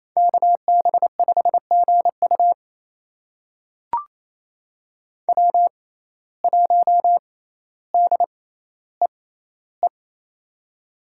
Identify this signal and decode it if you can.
Morse code, heard as a keyed tone, about 28 wpm, spaced as for 14 wpm, 700 Hz. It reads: KB5GU W1DEE
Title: USA Call Signs - 5 Characters - 6x Character Spacing - 28wpm